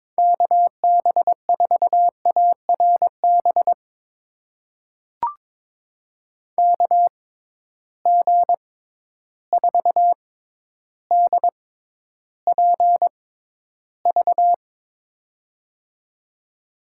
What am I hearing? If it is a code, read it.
Morse code, heard as a keyed tone, about 22 wpm, spaced as for 11 wpm, 700 Hz. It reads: KB4ARB KG4DPV
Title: USA Call Signs - 6 Characters - 6x Character Spacing - 22wpm